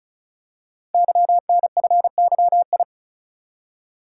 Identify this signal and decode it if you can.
Morse code, heard as a keyed tone, about 36 wpm, 700 Hz. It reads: YNFYI
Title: Code Groups - 5 Letters - 35wpm